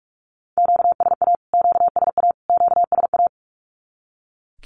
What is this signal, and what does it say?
Morse code, heard as a keyed tone, about 44 wpm, 700 Hz. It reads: QSA QSA QSA
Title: Q-Codes Repeated 3x 45wpm